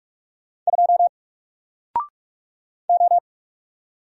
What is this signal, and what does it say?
Morse code, heard as a keyed tone, about 46 wpm, 700 Hz. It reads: 2 O
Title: Single Letter-Number 45wpm